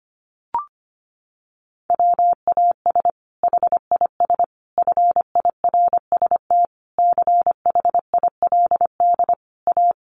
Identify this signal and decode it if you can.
Morse code, heard as a keyed tone, about 25 wpm, 700 Hz. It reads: WAS HIS FIRST CHILD A
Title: Sentences from Top 800 Words 25wpm